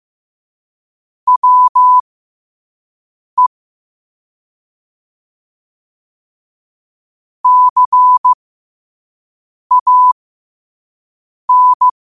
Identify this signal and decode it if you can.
Morse code, heard as a keyed tone, about 15 wpm, spaced as for 5 wpm, 1000 Hz. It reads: WE CAN
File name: qotdfiles / 5wpm